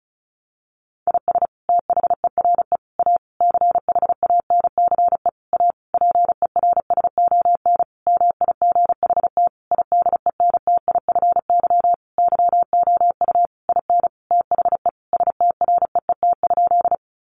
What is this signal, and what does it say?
Morse code, heard as a keyed tone, about 34 wpm, 700 Hz. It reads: IS THERE A CHANCE A PERSON MIGHT IDENTIFY YOU IN THE STREET?
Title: Sentences from Top 900 Words 35wpm